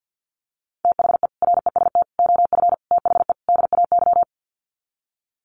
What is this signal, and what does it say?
Morse code, heard as a keyed tone, about 50 wpm, 700 Hz. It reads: THE REST OF THE DAY
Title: Sets of 5 Words 50wpm